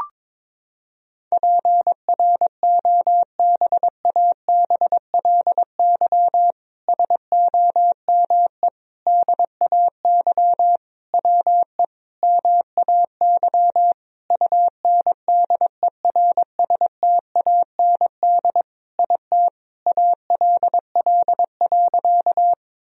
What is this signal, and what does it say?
Morse code, heard as a keyed tone, about 22 wpm, 700 Hz. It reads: PROBABLY SOME DAY WE MAY UNDERSTAND IT ALL.
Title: Sentences from Top 1000 Words - Review Part 4 - 22wpm